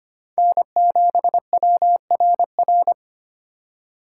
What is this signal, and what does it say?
Morse code, heard as a keyed tone, about 25 wpm, 700 Hz. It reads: N7WRR
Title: USA Call Signs - 4x Character Spacing - 25wpm